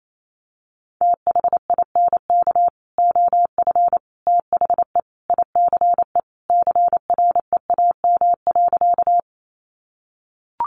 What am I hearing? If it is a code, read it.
Morse code, heard as a keyed tone, about 28 wpm, 700 Hz. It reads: THINK OF THE ICE CREAM.
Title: Sentences from Top 1500 Words 28wpm